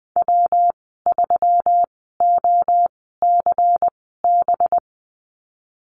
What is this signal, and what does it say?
Morse code, heard as a keyed tone, about 20 wpm, spaced as for 14 wpm, 700 Hz. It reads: W3OCB
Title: 193 Lesson 31g - Letter v - Call Signs 20wpm